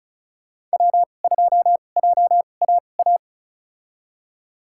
Morse code, heard as a keyed tone, about 36 wpm, spaced as for 25 wpm, 700 Hz. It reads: W2JAA
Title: USA Call Signs - 5 Characters - 2x Character Spacing - 35wpm